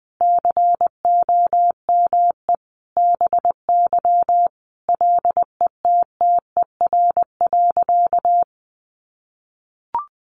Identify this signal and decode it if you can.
Morse code, heard as a keyed tone, about 20 wpm, 700 Hz. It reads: COME BY LETTER.
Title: Sentences from Top 1400 Words 20wpm